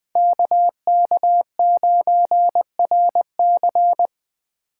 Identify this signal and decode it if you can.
Morse code, heard as a keyed tone, about 20 wpm, 700 Hz. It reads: KK9RC
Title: Speed Racing - USA 2x2 Call Signs 17wpm